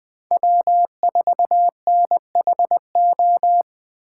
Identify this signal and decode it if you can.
Morse code, heard as a keyed tone, about 20 wpm, 700 Hz. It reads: W4NHO
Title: USA Call Signs - 6x Character Spacing - 20wpm